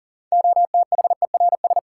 Morse code, heard as a keyed tone, about 39 wpm, 700 Hz. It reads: OTHERS
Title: Sentences from Top 1500 Words - New Word Drill Alt - 40wpm